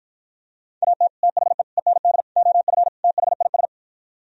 Morse code, heard as a keyed tone, about 52 wpm, 700 Hz. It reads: AT THE END OF THIS